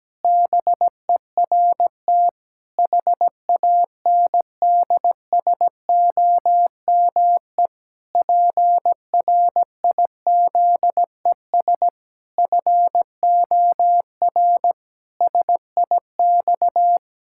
Morse code, heard as a keyed tone, about 17 wpm, 700 Hz. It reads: BERT HANDSOME PRIZES FOR SIX
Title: Pangram Challenge v2 - 17wpm